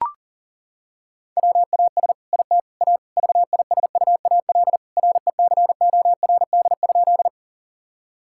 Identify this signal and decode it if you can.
Morse code, heard as a keyed tone, about 40 wpm, 700 Hz. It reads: WAS IT A VISUAL RECORD?
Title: Sentences from Top 1000 Words - Review Part 6 - 40wpm